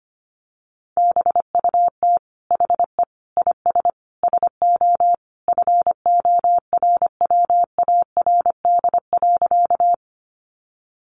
Morse code, heard as a keyed tone, about 25 wpm, 700 Hz. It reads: BUT HE IS SO FORWARD.
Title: Sentences from Top 1300 Words 25wpm